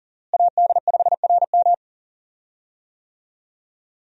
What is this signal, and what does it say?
Morse code, heard as a keyed tone, about 40 wpm, 700 Hz. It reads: AD5RM
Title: USA Call Signs - 5 Characters - Repeated 2x - 40wpm